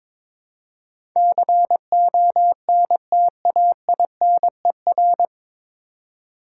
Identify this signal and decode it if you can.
Morse code, heard as a keyed tone, about 22 wpm, 700 Hz. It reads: CONTAINER
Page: Morse Code Ninja practice 22wpm